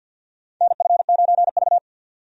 Morse code, heard as a keyed tone, about 53 wpm, 700 Hz. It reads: NF9V
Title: USA Call Signs - 4 Characters - 50wpm